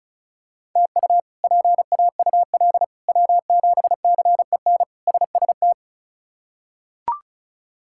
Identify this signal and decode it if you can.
Morse code, heard as a keyed tone, about 34 wpm, 700 Hz. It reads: TU PAUL W7CEN SST
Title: K1USN SST QSO Practice - 35wpm